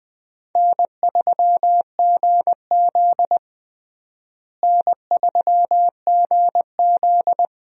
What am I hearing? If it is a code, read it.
Morse code, heard as a keyed tone, about 20 wpm, 700 Hz. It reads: N3GZ N3GZ
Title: USA Call Signs - 4 Characters - Repeated 3x - 20wpm